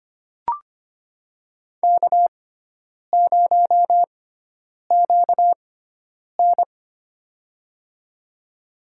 Morse code, heard as a keyed tone, about 25 wpm, spaced as for 9 wpm, 700 Hz. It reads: K0QN
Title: USA Call Signs - 4 Characters - 6x Character Spacing - 25wpm